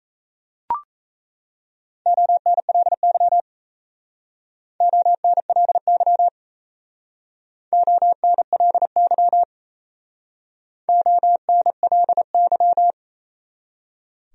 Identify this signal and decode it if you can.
Morse code, heard as a keyed tone, about 34 wpm, 700 Hz. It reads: ONLY ONLY ONLY ONLY
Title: Speed-Racing Top 500 Words 28wpm